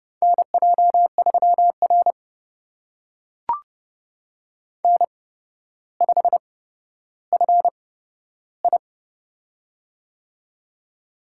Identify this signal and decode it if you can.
Morse code, heard as a keyed tone, about 30 wpm, spaced as for 13 wpm, 700 Hz. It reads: NJ3R N5FI
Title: USA Call Signs - 4 Characters - 8x Character Spacing - 30wpm